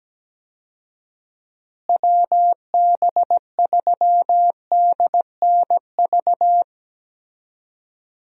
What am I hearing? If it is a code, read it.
Morse code, heard as a keyed tone, about 17 wpm, 700 Hz. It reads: WB3DNV
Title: USA Call Signs - 6 Characters - Repeated 2x - 17wpm